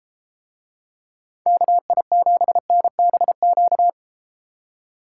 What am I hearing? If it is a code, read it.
Morse code, heard as a keyed tone, about 31 wpm, 700 Hz. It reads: KI7NBQ